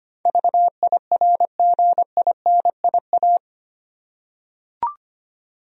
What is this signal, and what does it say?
Morse code, heard as a keyed tone, about 25 wpm, 700 Hz. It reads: VIRGINIA
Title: USA Common Names - 25wpm